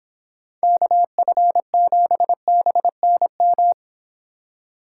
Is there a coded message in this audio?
Yes: Morse code from 0.6 to 3.7 s, heard as a keyed tone, about 29 wpm, 700 Hz. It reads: KF7BNM